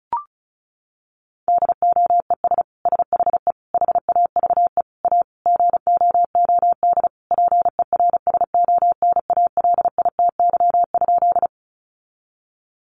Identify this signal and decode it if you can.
Morse code, heard as a keyed tone, about 34 wpm, 700 Hz. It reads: DOES SHE HAVE A GOOD PERSONALITY?
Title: Sentences from Top 1400 Words 35wpm